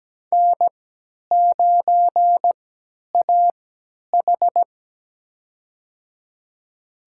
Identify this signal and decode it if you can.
Morse code, heard as a keyed tone, about 17 wpm, spaced as for 10 wpm, 700 Hz. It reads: N9AH
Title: USA Call Signs - 4 Characters - 3x Character Spacing - 17wpm